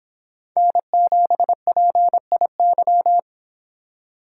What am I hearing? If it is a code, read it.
Morse code, heard as a keyed tone, about 25 wpm, 700 Hz. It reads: N7PIY